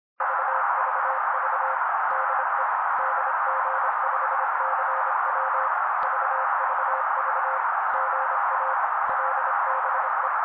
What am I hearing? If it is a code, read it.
Morse code, heard as a keyed tone, about 26 wpm, 530 Hz. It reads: VVV DE DG5CW VVV GA LBE
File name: daily / 25wpm noise